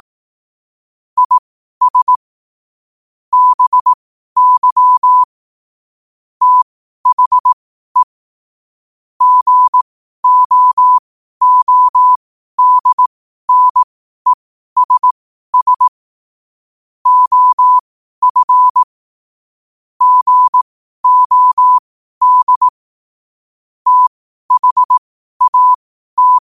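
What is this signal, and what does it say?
Morse code, heard as a keyed tone, about 18 wpm, spaced as for 12 wpm, 1000 Hz. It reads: IS BY THE GOODNESS OF GOD THAT
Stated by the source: 12 wpm Morse code.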